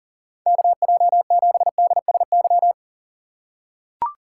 Morse code, heard as a keyed tone, about 40 wpm, 700 Hz. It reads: KJ7DSY
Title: USA Call Signs - 6 Characters - 4x Character Spacing - 40wpm